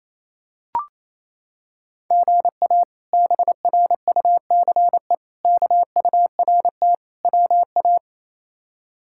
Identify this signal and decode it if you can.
Morse code, heard as a keyed tone, about 28 wpm, 700 Hz. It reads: GA BRUCE KURT WA
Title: K1USN SST QSO Practice - 28wpm